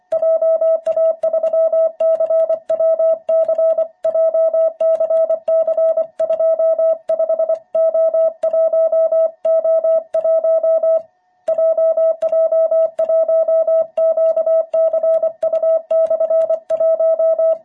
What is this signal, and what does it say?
Morse code, heard as a keyed tone, about 25 wpm, 660 Hz. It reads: JA3CWC JCC25O1O1 JJ1QCU/1
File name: jh8cbh / 2021cw